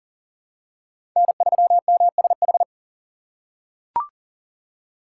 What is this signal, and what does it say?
Morse code, heard as a keyed tone, about 40 wpm, 700 Hz. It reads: N3MSH
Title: USA Call Signs - 5 Characters - 3x Character Spacing - 40wpm